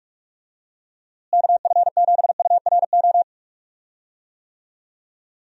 Morse code, heard as a keyed tone, about 46 wpm, 700 Hz. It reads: KF7URO
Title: USA Call Signs - 6 Characters - Repeated 2x - 45wpm